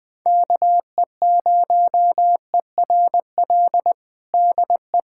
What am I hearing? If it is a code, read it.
Morse code, heard as a keyed tone, about 20 wpm, 700 Hz. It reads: KE0ERL DE
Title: USA Callsign DE Callsign 20wpm